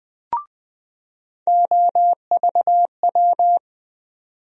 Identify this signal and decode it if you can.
Morse code, heard as a keyed tone, about 20 wpm, 700 Hz. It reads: OVW
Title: USA Call Sign Postfixes - 3 Letters 20wpm